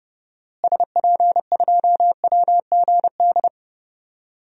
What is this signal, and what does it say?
Morse code, heard as a keyed tone, about 30 wpm, 700 Hz. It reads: SP2WGD
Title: DXCC-100 Call Signs - 6 Characters - Repeated 2x - 30wpm